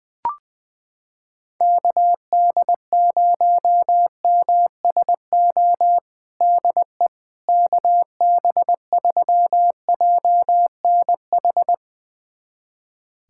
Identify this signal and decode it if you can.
Morse code, heard as a keyed tone, about 20 wpm, 700 Hz. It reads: KD0MSO DE KB3JNH
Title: USA Callsign DE Callsign - 6 Characters - 20wpm